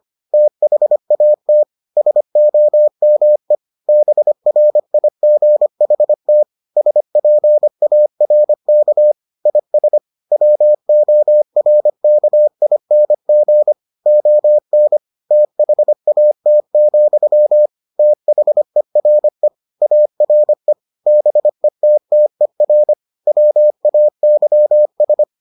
Morse code, heard as a keyed tone, about 25 wpm, 600 Hz. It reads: THAT SOME BRIGHT SPARK IS WORKING ON THAT, THERE ARE BETTER WAYS